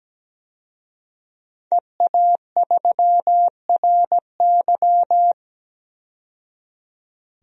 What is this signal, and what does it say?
Morse code, heard as a keyed tone, about 17 wpm, 700 Hz. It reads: EA3RY
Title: DXCC-100 Call Signs - 5 Characters - 17wpm